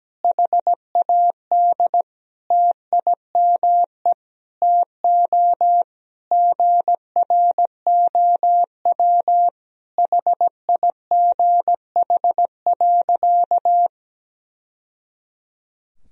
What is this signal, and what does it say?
Morse code, heard as a keyed tone, about 17 wpm, 700 Hz. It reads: HAD TIME TO GROW HIGH.
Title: Sentences from Top 1000 Words - Review Part 1 - 17wpm